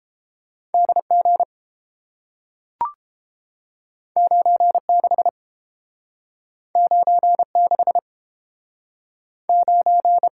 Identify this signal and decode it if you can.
Morse code, heard as a keyed tone, about 30 wpm, 700 Hz. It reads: D G 9 6 9 6 9